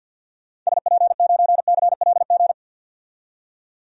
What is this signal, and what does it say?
Morse code, heard as a keyed tone, about 52 wpm, 700 Hz. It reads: SP9CLG
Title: DXCC-100 Call Signs - 6 Characters - 50wpm